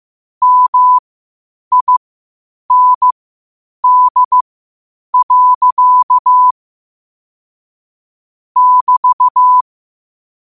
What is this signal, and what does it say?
Morse code, heard as a keyed tone, about 15 wpm, spaced as for 8 wpm, 1000 Hz. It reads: MIND. <BT>
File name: qotdfiles / 8wpm